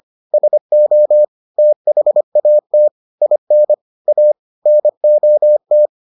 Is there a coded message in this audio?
Yes: Morse code from 0.3 to 5.9 s, heard as a keyed tone, about 25 wpm, 600 Hz. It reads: SO THAT IN A NOT